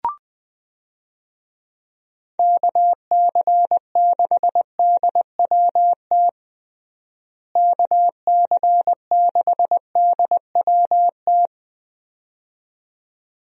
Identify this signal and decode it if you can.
Morse code, heard as a keyed tone, about 21 wpm, 700 Hz. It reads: KC6DWT KC6DWT
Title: USA Call Signs - 6 Characters - Repeated 2x - 20wpm